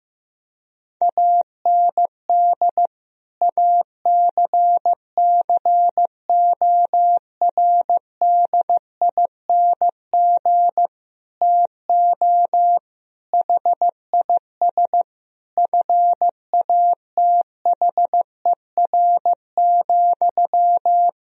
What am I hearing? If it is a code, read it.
Morse code, heard as a keyed tone, about 15 wpm, 700 Hz. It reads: AND ACCORDING TO HIS FATHER,
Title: Sentences from Top 1000 Words - Review Part 4 - 15wpm